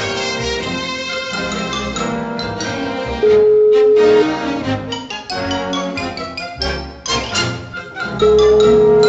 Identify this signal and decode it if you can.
Ringback tone (Europe, 425 Hz), with noise.